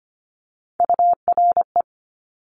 Morse code, heard as a keyed tone, about 25 wpm, 700 Hz. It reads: URE
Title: Top Three Letter Combinations in Words 25wpm